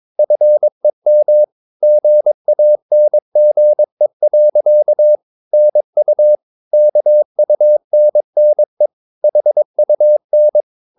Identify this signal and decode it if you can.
Morse code, heard as a keyed tone, about 22 wpm, 600 Hz. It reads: FEM GANGE. NU KUNNE HUN
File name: Ferie QRS 22wpm